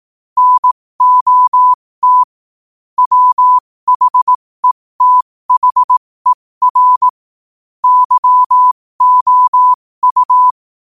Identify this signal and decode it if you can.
Morse code, heard as a keyed tone, about 18 wpm, spaced as for 15 wpm, 1000 Hz. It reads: NOT WHETHER YOU
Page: 15 WPM morse code quotes for Tue, 23 Sep 2025.